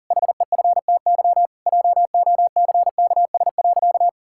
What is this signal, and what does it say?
Morse code, heard as a keyed tone, about 40 wpm, 700 Hz. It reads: HEFTY JOCKS.
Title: Pangram Challenge v2 - 40wpm